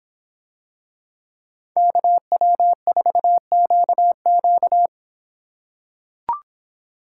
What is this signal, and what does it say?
Morse code, heard as a keyed tone, about 25 wpm, 700 Hz. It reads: KW4QQ